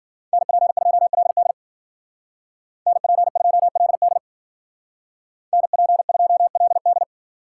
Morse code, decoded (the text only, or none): NP2LD NP2LD NP2LD